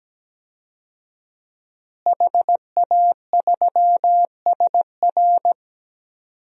Morse code, heard as a keyed tone, about 17 wpm, 700 Hz. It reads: HA3SR
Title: DXCC-100 Call Signs - 5 Characters - Repeated 3x - 17wpm